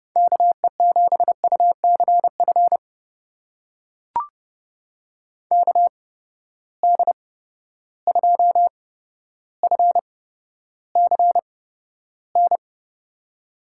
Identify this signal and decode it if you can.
Morse code, heard as a keyed tone, about 30 wpm, spaced as for 13 wpm, 700 Hz. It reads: KE7UCF KD2FCN
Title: USA Call Signs - 6 Characters - 8x Character Spacing - 30wpm